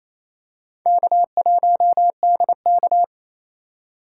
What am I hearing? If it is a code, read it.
Morse code, heard as a keyed tone, about 28 wpm, 700 Hz. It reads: K1DK
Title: USA Call Signs - 4 Characters - 2x Character Spacing - 28wpm